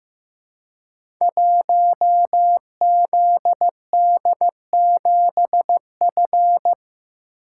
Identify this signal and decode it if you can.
Morse code, heard as a keyed tone, about 15 wpm, 700 Hz. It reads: 1ZD7F
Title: Code Groups - 5 Letter:Numbers - 15wpm